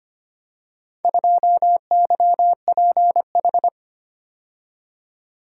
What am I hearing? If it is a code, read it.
Morse code, heard as a keyed tone, about 25 wpm, 700 Hz. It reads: 2YPH
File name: practice / 25wpm / Four Letter-Number Combinations - 25wpm